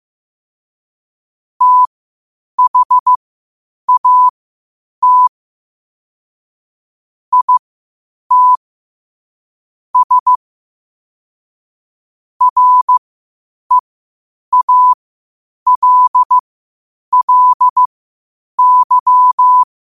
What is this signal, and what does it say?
Morse code, heard as a keyed tone, about 15 wpm, spaced as for 8 wpm, 1000 Hz. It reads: THAT IT S REALLY
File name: 8wpm